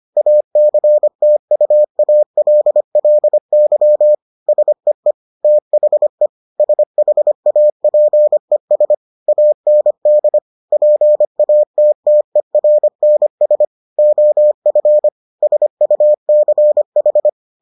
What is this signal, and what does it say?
Morse code, heard as a keyed tone, about 25 wpm, 600 Hz. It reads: ACTUALLY SEE THE SHAPES AND PATTERNS OF SUCH